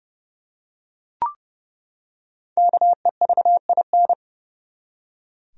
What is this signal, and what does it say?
Morse code, heard as a keyed tone, about 32 wpm, 700 Hz. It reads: KEVIN
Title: USA Common Names - Max 5 Letters - 30wpm